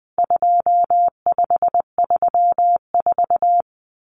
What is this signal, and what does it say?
Morse code, heard as a keyed tone, about 20 wpm, 700 Hz. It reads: 2534
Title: Numbers - 4-Digit 20wpm